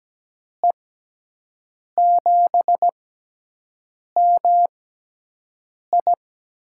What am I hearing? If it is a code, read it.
Morse code, heard as a keyed tone, about 17 wpm, spaced as for 6 wpm, 700 Hz. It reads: E7MI
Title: USA Call Signs - 5 Characters - 6x Character Spacing - 17wpm